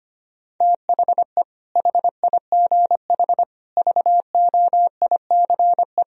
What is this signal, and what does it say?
Morse code, heard as a keyed tone, about 25 wpm, 700 Hz. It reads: THE HIGH VOICE
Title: Sentences from Top 1000 Words - Review Part 1 - 25wpm